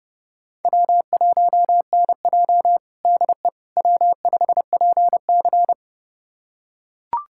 Morse code, heard as a keyed tone, about 30 wpm, 700 Hz. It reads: W1NJ DE W5PC
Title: USA Callsign DE Callsign - 4 Characters - 30wpm